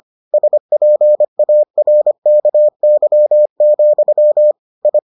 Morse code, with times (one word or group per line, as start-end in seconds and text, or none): 0.3-4.5 SPARKY,
4.9-5.0 I